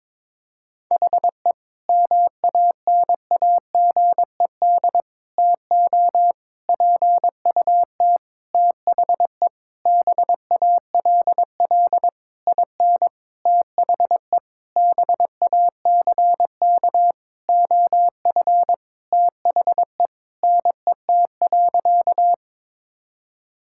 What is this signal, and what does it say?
Morse code, heard as a keyed tone, about 22 wpm, 700 Hz. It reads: HE MANAGED TO PUT THE BALL IN THE BACK OF THE NET.
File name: Sentences with 9-Word Phrases - 22wpm